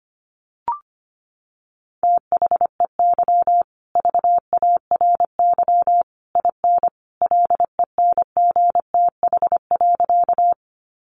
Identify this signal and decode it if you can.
Morse code, heard as a keyed tone, about 25 wpm, 700 Hz. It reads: THEY VARY IN LENGTH.
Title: Sentences from Top 1200 Words 25wpm